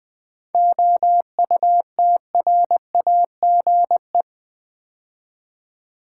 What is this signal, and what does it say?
Morse code, heard as a keyed tone, about 20 wpm, 700 Hz. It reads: OUTRAGE
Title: 7 Letter Words - First 3 Letters Given - 20wpm